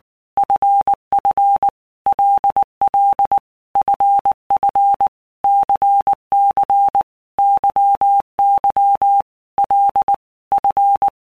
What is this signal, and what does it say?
Morse code, heard as a keyed tone, about 19 wpm, 800 Hz. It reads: FF LL FF CC YY L F
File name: Basic Letter Groups By Blocks / Groups@18 / LFCYQ@18